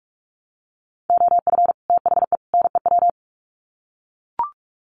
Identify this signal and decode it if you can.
Morse code, heard as a keyed tone, about 44 wpm, 700 Hz. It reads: OF THE NEW
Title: Sets of 3 Words 45wpm